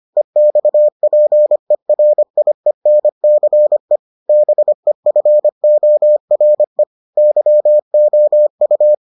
Morse code, heard as a keyed tone, about 25 wpm, 600 Hz. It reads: EXPERIENCE BEFORE YOU